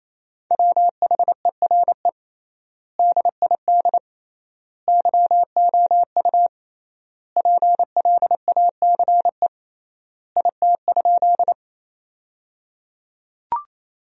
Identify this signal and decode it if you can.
Morse code, heard as a keyed tone, about 28 wpm, 700 Hz. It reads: WHERE DID YOU PLACE IT?
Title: 3x Word-Spacing - Sentences from Top 200 Words - 28wpm